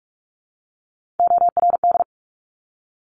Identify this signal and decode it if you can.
Morse code, heard as a keyed tone, about 46 wpm, 690 Hz. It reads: ORD
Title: Top Three Letter Combinations in Words 45wpm